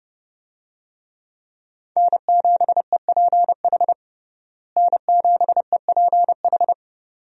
Morse code, heard as a keyed tone, about 32 wpm, 700 Hz. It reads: N7EPH N7EPH
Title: USA Call Signs - 5 Characters - Repeated 2x - 30wpm